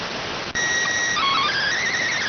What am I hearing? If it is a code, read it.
SSTV (slow-scan television), PD 290.